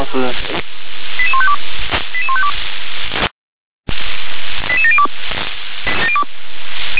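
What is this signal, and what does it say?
Five-tone selective call (ZVEI), heard as quick runs of five tones, tones of about 70 ms, with speech. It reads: 88152 88152 88702
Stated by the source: ZVEI five-tone selective call received on an SDR.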